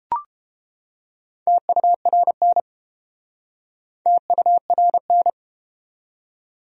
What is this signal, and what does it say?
Morse code, heard as a keyed tone, about 32 wpm, 700 Hz. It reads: TURN TURN